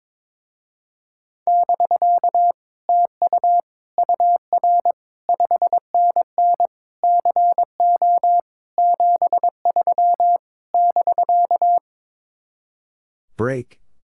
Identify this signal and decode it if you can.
Morse code, heard as a keyed tone, about 22 wpm, 700 Hz. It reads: <BK> TU UR 5NN CO 73 <BK>
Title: Speed-Racing Common QSO Phrases - 22wpm